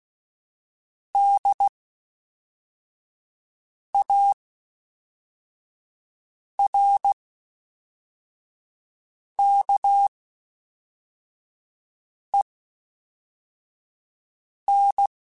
Morse code, heard as a keyed tone, about 16 wpm, 800 Hz. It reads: D A R K E N
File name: Special INGs / Special INGs-mp3s / Fast ing @10